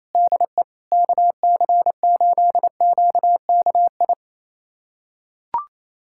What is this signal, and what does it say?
Morse code, heard as a keyed tone, about 28 wpm, 700 Hz. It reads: DE KC8QKI
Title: USA Callsign DE Callsign 28wpm